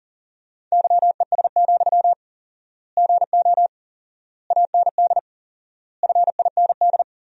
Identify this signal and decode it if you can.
Morse code, heard as a keyed tone, about 40 wpm, 700 Hz. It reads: YES, GO AND FIND
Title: 4x Word-Spacing - Sentences from Top 500 Words - 40wpm